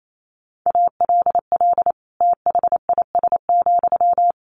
Morse code, heard as a keyed tone, about 28 wpm, 700 Hz. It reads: ALL THIS,
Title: Sentences from Top 600 Words 28wpm